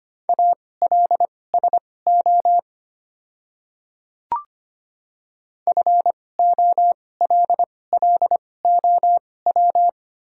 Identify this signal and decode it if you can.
Morse code, heard as a keyed tone, about 25 wpm, spaced as for 18 wpm, 700 Hz. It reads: ALSO FOLLOW
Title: 2x Character Spacing - Top 200 Words - 25wpm